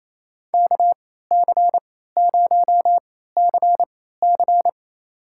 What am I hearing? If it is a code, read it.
Morse code, heard as a keyed tone, about 28 wpm, spaced as for 16 wpm, 700 Hz. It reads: KC0CC
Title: USA Call Signs - 3x Character Spacing - 28wpm